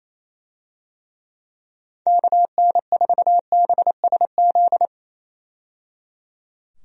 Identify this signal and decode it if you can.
Morse code, heard as a keyed tone, about 27 wpm, 700 Hz. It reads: KN4BSZ
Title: USA Call Signs - 6 Characters - 28wpm